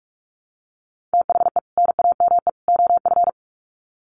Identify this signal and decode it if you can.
Morse code, heard as a keyed tone, about 44 wpm, 700 Hz. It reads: THE NAME OF
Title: Sets of 3 Words 45wpm